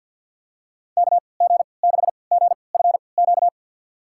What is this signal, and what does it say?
Morse code, heard as a keyed tone, about 48 wpm, spaced as for 28 wpm, 700 Hz. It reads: KG6GFQ
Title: USA Call Signs - 6 Characters - 3x Character Spacing - 50wpm